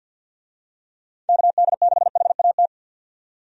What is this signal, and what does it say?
Morse code, heard as a keyed tone, about 52 wpm, 700 Hz. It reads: KD6HAT
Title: USA Call Signs - 6 Characters - 50wpm